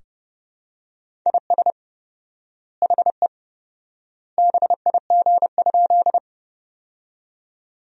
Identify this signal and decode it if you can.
Morse code, heard as a keyed tone, about 31 wpm, 700 Hz. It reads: IS HE BIG?
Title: 4x Word-Spacing - Sentences from Top 200 Words - 30wpm